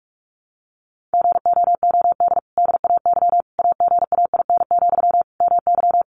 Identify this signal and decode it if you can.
Morse code, heard as a keyed tone, about 44 wpm, 700 Hz. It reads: GOOD DAY AGAIN, MY
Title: Sentences from Top 1100 Words 45wpm